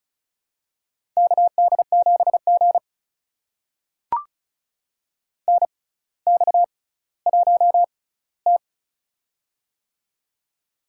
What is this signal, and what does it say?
Morse code, heard as a keyed tone, about 34 wpm, spaced as for 18 wpm, 700 Hz. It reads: KD7G NX1T
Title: USA Call Signs - 4 Characters - 6x Character Spacing - 35wpm